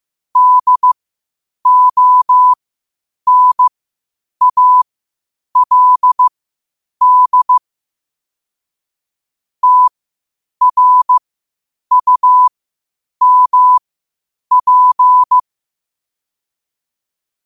Morse code, heard as a keyed tone, about 15 wpm, spaced as for 8 wpm, 1000 Hz. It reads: DONALD TRUMP
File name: qotdfiles / 8wpm